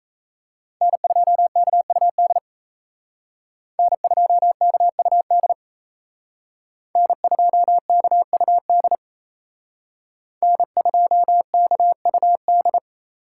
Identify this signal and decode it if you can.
Morse code, heard as a keyed tone, about 35 wpm, 700 Hz. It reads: N2KUD N2KUD N2KUD N2KUD